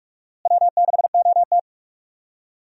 Morse code, heard as a keyed tone, about 46 wpm, 700 Hz. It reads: W6OT
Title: USA Call Signs - 4 Characters - Repeated 3x - 45wpm